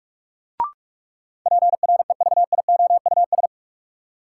Morse code, heard as a keyed tone, about 46 wpm, 700 Hz. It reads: PREVIOUS
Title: Sentences from Top 1500 Words - New Word Drill Alt - 45wpm